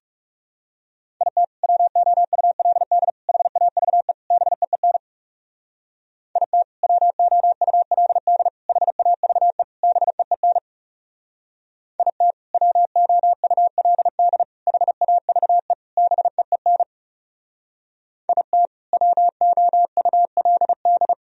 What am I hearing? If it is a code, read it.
Morse code, heard as a keyed tone, about 37 wpm, 700 Hz. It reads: IT WOULD HAVE BEEN IT WOULD HAVE BEEN IT WOULD HAVE BEEN IT WOULD
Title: Speed-Racing - Sets of 4 Words 30wpm